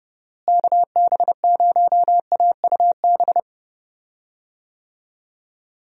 Morse code, heard as a keyed tone, about 30 wpm, 700 Hz. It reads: KB0AUB
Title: USA Call Signs - 6 Characters - Repeated 3x - 30wpm